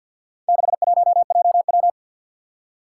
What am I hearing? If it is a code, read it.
Morse code, heard as a keyed tone, about 51 wpm, 700 Hz. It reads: 61JW
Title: Four Challenging Letter-Number Combinations 50wpm